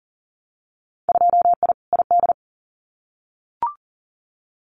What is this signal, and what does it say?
Morse code, heard as a keyed tone, about 40 wpm, 700 Hz. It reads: 2I ID
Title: Winter Field Day Exchanges Encore 40wpm